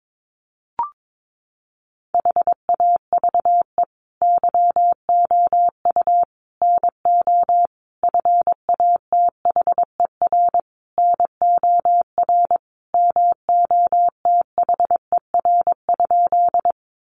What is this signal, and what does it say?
Morse code, heard as a keyed tone, about 22 wpm, 700 Hz. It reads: HAVE YOU NO FATHER NOR MOTHER?
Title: Sentences from Top 1500 Words 22wpm